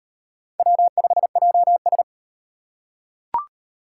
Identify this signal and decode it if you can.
Morse code, heard as a keyed tone, about 37 wpm, 700 Hz. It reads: W5JS